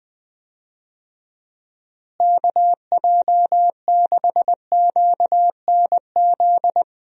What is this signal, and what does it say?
Morse code, heard as a keyed tone, about 20 wpm, 700 Hz. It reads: KJ6QNZ
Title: USA Call Signs - 6 Characters - Repeated 2x - 20wpm